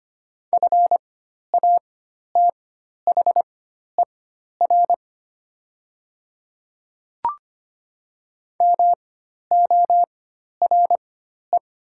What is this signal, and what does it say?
Morse code, heard as a keyed tone, about 25 wpm, spaced as for 12 wpm, 690 Hz. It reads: FATHER MORE
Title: 4x Character Spacing - Top 200 Words - 25wpm